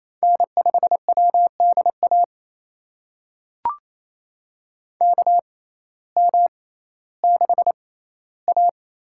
Morse code, heard as a keyed tone, about 28 wpm, spaced as for 15 wpm, 700 Hz. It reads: N5WDA KM6A
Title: USA Call Signs - 6x Character Spacing - 28wpm